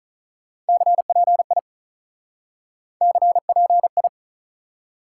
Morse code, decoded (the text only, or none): CPI CPI